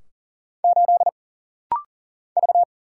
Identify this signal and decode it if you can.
Morse code, heard as a keyed tone, about 39 wpm, 700 Hz. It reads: 8 V
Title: Single Letter-Number Alt - Rapid-Fire 40wpm